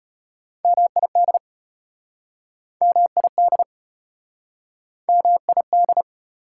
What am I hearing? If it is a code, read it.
Morse code, heard as a keyed tone, about 34 wpm, 700 Hz. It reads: MID MID MID